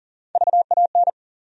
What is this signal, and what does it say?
Morse code, heard as a keyed tone, about 40 wpm, 700 Hz. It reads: VAN
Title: USA Call Sign Postfixes - 3 Letters 40wpm